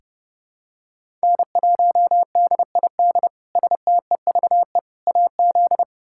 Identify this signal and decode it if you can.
Morse code, heard as a keyed tone, about 30 wpm, 700 Hz. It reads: N1DID STEVE AZ
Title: K1USN SST QSO Practice - 30wpm